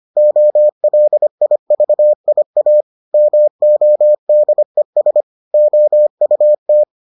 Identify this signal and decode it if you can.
Morse code, heard as a keyed tone, about 25 wpm, 600 Hz. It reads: OLIVIA MODES OUT